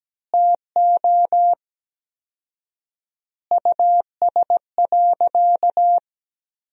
Morse code, heard as a keyed tone, about 17 wpm, 700 Hz. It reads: TO US.
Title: 4x Word-Spacing - Sentences from Top 200 Words - 17wpm